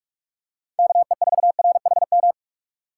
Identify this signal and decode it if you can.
Morse code, heard as a keyed tone, about 44 wpm, 700 Hz. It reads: KE4RHM